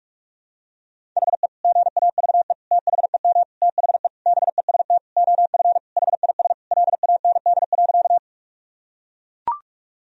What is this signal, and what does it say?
Morse code, heard as a keyed tone, about 41 wpm, 700 Hz. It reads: HE GAVE THEM THE BEST OF HIS LAND.
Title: Sentences from Top 1000 Words - Review Part 1 - 45wpm